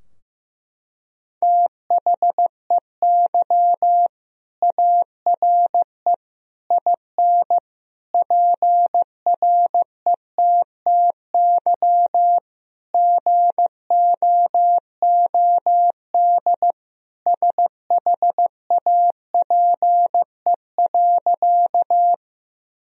Morse code, heard as a keyed tone, about 15 wpm, 700 Hz. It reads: THEY ARE IN PRETTY GOOD SHAPE.
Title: Sentences from Top 1000 Words - Review Part 3 - 15wpm